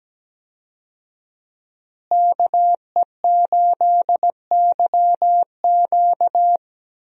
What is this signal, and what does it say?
Morse code, heard as a keyed tone, about 17 wpm, 700 Hz. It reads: KE8YQ
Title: USA Call Signs - 5 Characters - Repeated 2x - 17wpm